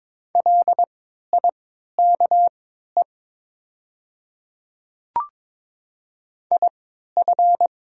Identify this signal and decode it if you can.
Morse code, heard as a keyed tone, about 22 wpm, spaced as for 13 wpm, 700 Hz. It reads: LIKE IF
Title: 3x Character Spacing - Top 100 Words - 22wpm